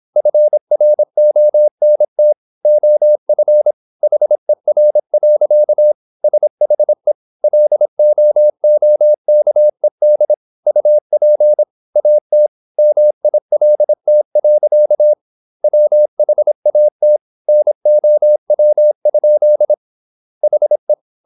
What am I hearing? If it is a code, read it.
Morse code, heard as a keyed tone, about 26 wpm, 600 Hz. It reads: FRONT OF HER. SHE LOOKED UP AT MILT. WHAT NOW? HE
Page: Never 26wpm | CW med Gnister